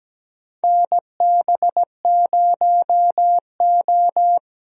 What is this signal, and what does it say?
Morse code, heard as a keyed tone, about 17 wpm, 700 Hz. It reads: NB0O
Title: USA Call Signs - 4 Characters - Repeated 2x - 17wpm